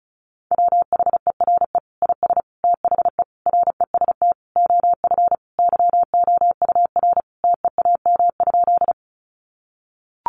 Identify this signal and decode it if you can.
Morse code, heard as a keyed tone, about 34 wpm, 700 Hz. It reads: WHERE IS THE REST OF YOUR TEAM?
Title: Sentences from Top 1500 Words 35wpm